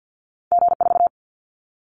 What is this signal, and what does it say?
Morse code, heard as a keyed tone, about 50 wpm, 700 Hz. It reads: G4
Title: Callsign Prefixes for Top 25 DXCC - 50wpm